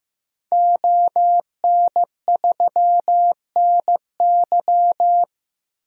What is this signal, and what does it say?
Morse code, heard as a keyed tone, about 15 wpm, 700 Hz. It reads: ON3NY
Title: DXCC-100 Call Signs - 5 Characters - Repeated 3x - 15wpm